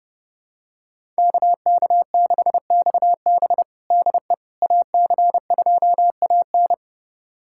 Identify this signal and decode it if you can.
Morse code, heard as a keyed tone, about 30 wpm, 700 Hz. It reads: KK6XB DE AC2AN
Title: USA Callsign DE Callsign - 5 Characters - 30wpm